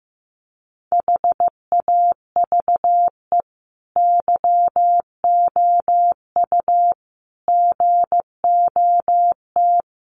Morse code, heard as a keyed tone, about 15 wpm, 700 Hz. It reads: HAVE YOU GOT
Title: Sentences from Top 300 Words 15wpm